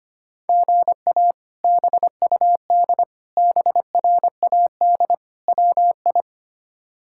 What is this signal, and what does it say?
Morse code, heard as a keyed tone, about 25 wpm, 700 Hz. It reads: GA BUD BRAD WI
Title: K1USN SST QSO Practice - 25wpm